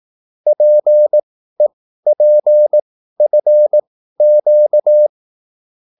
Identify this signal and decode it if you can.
Morse code, heard as a keyed tone, about 18 wpm, spaced as for 13 wpm, 600 Hz. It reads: PEPFQ